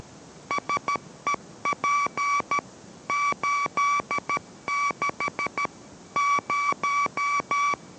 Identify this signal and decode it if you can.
Morse code, heard as a keyed tone, about 13 wpm, 1140 Hz. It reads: SEP 86 0